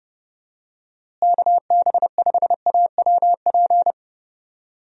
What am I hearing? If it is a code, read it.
Morse code, heard as a keyed tone, about 30 wpm, 700 Hz. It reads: KB5AWP
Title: USA Call Signs - 6 Characters - Repeated 2x - 30wpm